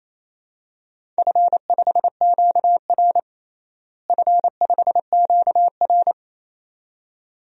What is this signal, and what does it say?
Morse code, heard as a keyed tone, about 28 wpm, 700 Hz. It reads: F5QR F5QR
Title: DXCC-100 Call Signs - 4 Characters - Repeated 2x - 28wpm